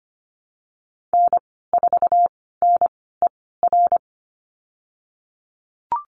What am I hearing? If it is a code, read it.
Morse code, heard as a keyed tone, about 25 wpm, spaced as for 16 wpm, 700 Hz. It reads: N4NER
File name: Lesson 15g - Number 5 - Call Signs 25wpm